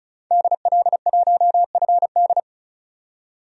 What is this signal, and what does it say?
Morse code, heard as a keyed tone, about 36 wpm, 700 Hz. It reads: DL1FD
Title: DXCC-100 Call Signs - 5 Characters - 35wpm